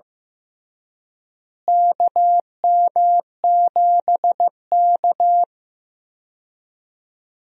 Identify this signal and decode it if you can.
Morse code, heard as a keyed tone, about 15 wpm, 700 Hz. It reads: KM7K
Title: USA Call Signs - 4 Characters - Repeated 3x - 15wpm